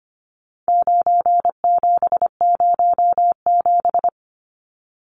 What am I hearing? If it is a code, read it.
Morse code, heard as a keyed tone, about 25 wpm, 700 Hz. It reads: 9707
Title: Numbers - 4-Digit 25wpm